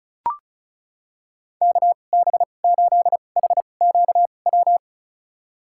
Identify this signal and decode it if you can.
Morse code, heard as a keyed tone, about 34 wpm, spaced as for 25 wpm, 700 Hz. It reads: KB8HQW
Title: USA Call Signs - 6 Characters - 2x Character Spacing - 35wpm